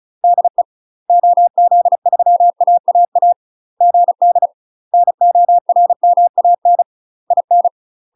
Morse code, heard as a keyed tone, about 34 wpm, 700 Hz. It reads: DE OZ3AAA GD NORMAN IN
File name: UK MASTER QSOs TRAINING / 35WPM / UK MASTER QSOs 0025